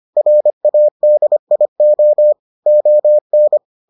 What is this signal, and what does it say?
Morse code, heard as a keyed tone, about 25 wpm, 600 Hz. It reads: RADIO ON